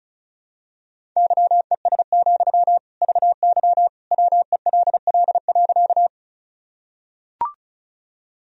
Morse code, heard as a keyed tone, about 36 wpm, 700 Hz. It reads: YES, VY WELL.
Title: Sentences from Top 500 Words and Abbreviations 35wpm